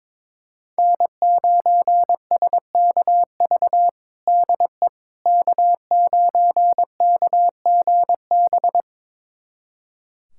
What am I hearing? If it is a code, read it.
Morse code, heard as a keyed tone, about 22 wpm, 700 Hz. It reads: N9SKV DE K9KGB
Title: USA Callsign DE Callsign - 5 Characters - 22wpm